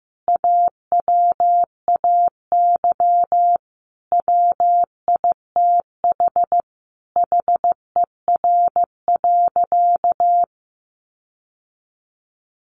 Morse code, heard as a keyed tone, about 14 wpm, 700 Hz. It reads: AWAY WITH HER.
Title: Sentences from Top 1500 Words 15wpm